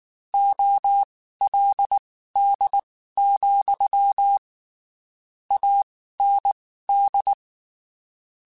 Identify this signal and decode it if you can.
Morse code, heard as a keyed tone, about 19 wpm, spaced as for 14 wpm, 800 Hz. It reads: OLD, AND
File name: (spaced) / QSOs Spaced-mp3s / WA7QOX 2 @18 sp